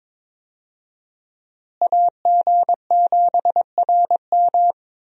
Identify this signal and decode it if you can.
Morse code, heard as a keyed tone, about 21 wpm, 700 Hz. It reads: AG7RM
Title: USA Call Signs - 5 Characters - Repeated 3x - 22wpm